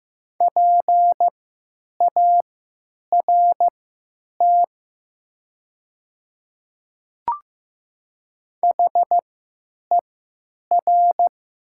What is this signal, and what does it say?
Morse code, heard as a keyed tone, about 15 wpm, spaced as for 9 wpm, 700 Hz. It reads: PART HER
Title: 3x Character Spacing - Top 200 Words - 15wpm